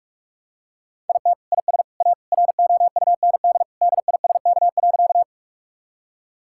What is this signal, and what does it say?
Morse code, heard as a keyed tone, about 44 wpm, 700 Hz. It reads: IT IS A ROUND DISK.
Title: Sentences from Top 1000 Words - Review Part 6 - 45wpm